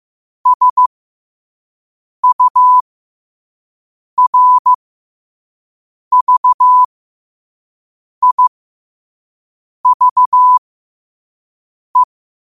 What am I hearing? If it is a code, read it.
Morse code, heard as a keyed tone, about 15 wpm, spaced as for 5 wpm, 1000 Hz. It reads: SURVIVE
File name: qotdfiles / 5wpm